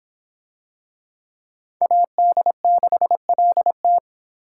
Morse code, heard as a keyed tone, about 23 wpm, 700 Hz. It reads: AD6LT